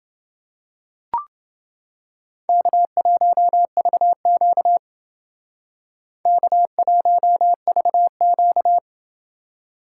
Morse code, heard as a keyed tone, about 28 wpm, 700 Hz. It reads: K1VQ K1VQ